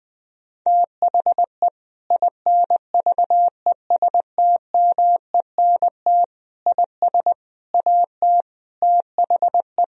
Morse code, heard as a keyed tone, about 20 wpm, 700 Hz. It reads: THE INVESTMENT IS AT THE
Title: Sentences from Top 1000 Words - Review Part 4 - 20wpm